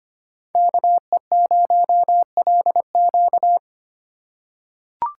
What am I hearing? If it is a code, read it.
Morse code, heard as a keyed tone, about 25 wpm, 700 Hz. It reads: KE0LQ
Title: USA Call Signs - 5 Characters - 2x Character Spacing - 25wpm